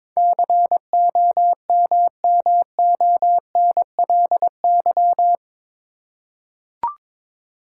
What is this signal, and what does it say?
Morse code, heard as a keyed tone, about 22 wpm, 700 Hz. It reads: COMMONLY
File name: practice / 22wpm / Eight Letter Words - 22wpm